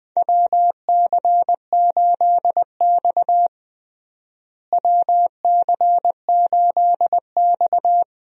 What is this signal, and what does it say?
Morse code, heard as a keyed tone, about 20 wpm, 700 Hz. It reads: WC8X WC8X
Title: USA Call Signs - 4 Characters - Repeated 3x - 20wpm